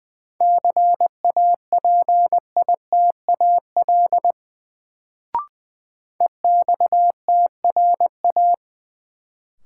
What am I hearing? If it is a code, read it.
Morse code, heard as a keyed tone, about 21 wpm, 700 Hz. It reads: CAPITAL EXTRA
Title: Sentences from Top 1000 Words - New Word Drill - 20wpm